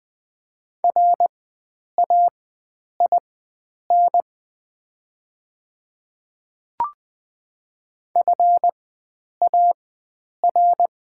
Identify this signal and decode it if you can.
Morse code, heard as a keyed tone, about 20 wpm, spaced as for 9 wpm, 700 Hz. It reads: RAIN FAR
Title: 4x Character Spacing - Top 300 Words - 20wpm